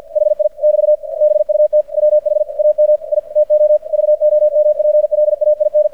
Morse code, heard as a keyed tone, about 50 wpm, 600 Hz, with radio noise. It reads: HI <BT> BUT WH AM I TO VOMPLAIN
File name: archived / CRSnet